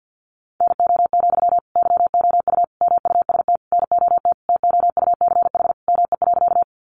Morse code, heard as a keyed tone, about 53 wpm, 700 Hz. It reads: NO, YOU MUST NOT TOUCH ME.
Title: Sentences from Top 1200 Words 50wpm